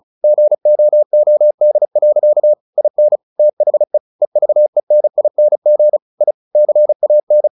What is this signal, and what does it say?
Morse code, heard as a keyed tone, about 34 wpm, 600 Hz. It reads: GOOD. IN THE EVENING I CAN
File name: the old man and the sea / the old man and the sea 1 35wpm